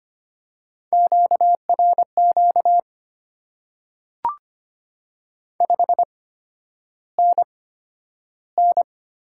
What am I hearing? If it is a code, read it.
Morse code, heard as a keyed tone, about 25 wpm, spaced as for 11 wpm, 700 Hz. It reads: QRQ 5NN
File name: Top 90 QSO Elements - 8x Character Spacing - 25wpm